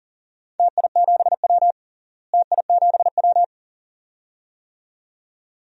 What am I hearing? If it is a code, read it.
Morse code, heard as a keyed tone, about 40 wpm, 700 Hz. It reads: TI7W TI7W
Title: DXCC-100 Call Signs - 4 Characters - Repeated 2x - 40wpm